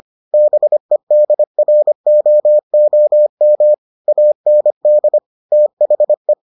Morse code, heard as a keyed tone, about 25 wpm, 600 Hz. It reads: BEDROOM AND THE